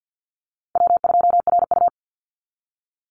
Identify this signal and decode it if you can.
Morse code, heard as a keyed tone, about 48 wpm, 700 Hz. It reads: W2RU
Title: Call Signs Repeated 3x - 50wpm